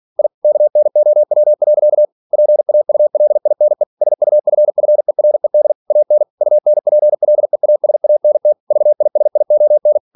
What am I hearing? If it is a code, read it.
Morse code, heard as a keyed tone, about 46 wpm, 600 Hz. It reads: I KNOW. PAULINE SUFFERED AN UNPLEASANT VISION
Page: Never 47wpm